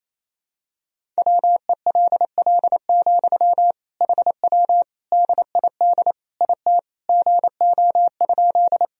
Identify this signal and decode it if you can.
Morse code, heard as a keyed tone, about 28 wpm, 700 Hz. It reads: WELL, HW DID IT GO?
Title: Sentences from Top 200 Words and Abbreviations 28wpm